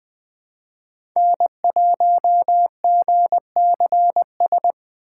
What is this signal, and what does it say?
Morse code, heard as a keyed tone, about 20 wpm, 700 Hz. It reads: N1GCS
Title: USA Call Signs - 5 Characters - 3x Character Spacing - 20wpm